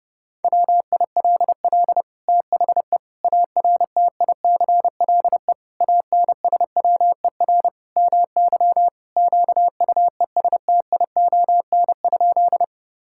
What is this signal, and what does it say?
Morse code, heard as a keyed tone, about 32 wpm, 700 Hz. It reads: WILL THE ARTICLE ANSWER MY QUESTION?
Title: Sentences from Top 1000 Words - Review Part 5 - 30wpm